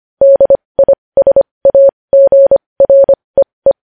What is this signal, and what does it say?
Morse code, heard as a keyed tone, about 25 wpm, spaced as for 20 wpm, 550 Hz. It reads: DISAGREE
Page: News Headlines in Morse Code at 20 WPM